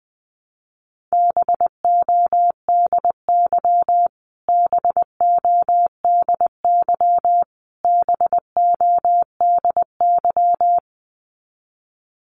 Morse code, heard as a keyed tone, about 20 wpm, 700 Hz. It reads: BODY BODY BODY
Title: Top 500 Words Repeated 20wpm